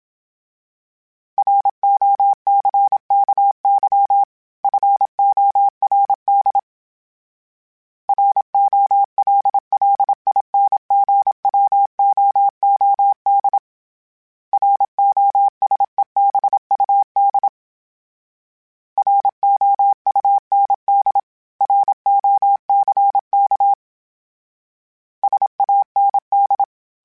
Morse code, heard as a keyed tone, about 27 wpm, 800 Hz. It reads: ROCKY FORD ROLLINGWOOD ROSEBUD ROUND ROCK SAND
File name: fromSMs / FromSMs mp3s / Cities@25